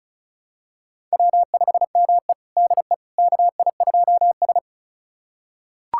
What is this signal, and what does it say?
Morse code, heard as a keyed tone, about 34 wpm, 700 Hz. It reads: W5ME DE KI2S
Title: USA Callsign DE Callsign - 4 Characters - 35wpm